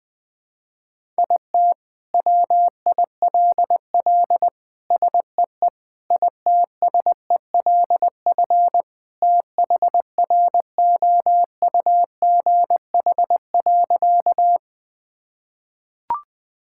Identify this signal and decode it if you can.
Morse code, heard as a keyed tone, about 20 wpm, 700 Hz. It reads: IT WILL SEE ITSELF THROUGH.
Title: Sentences from Top 1000 Words - Review Part 5 - 20wpm